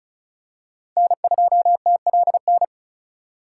Morse code, heard as a keyed tone, about 36 wpm, 700 Hz. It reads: N2TLN
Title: USA Call Signs - 5 Characters - 4x Character Spacing - 35wpm